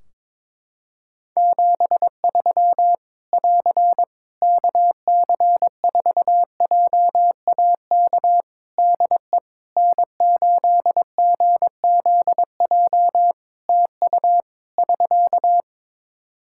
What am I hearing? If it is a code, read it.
Morse code, heard as a keyed tone, about 22 wpm, 700 Hz. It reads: 73 <AR> KC4JAK DE N8GZJ TU <SK>
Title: Practice QSO - Exchange 4 - Ending - 22wpm